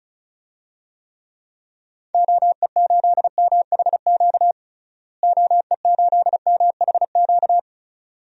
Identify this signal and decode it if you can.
Morse code, heard as a keyed tone, about 36 wpm, 700 Hz. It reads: OE8MHQ OE8MHQ
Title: DXCC-100 Call Signs - 6 Characters - Repeated 2x - 35wpm